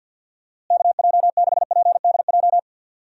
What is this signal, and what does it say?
Morse code, heard as a keyed tone, about 47 wpm, 700 Hz. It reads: KJ6PDJ
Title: USA Call Signs - 6 Characters - Repeated 2x - 50wpm